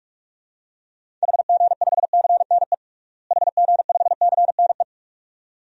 Morse code, heard as a keyed tone, about 46 wpm, 700 Hz. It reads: HG5CNE HG5CNE
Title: DXCC-100 Call Signs - 6 Characters - Repeated 2x - 45wpm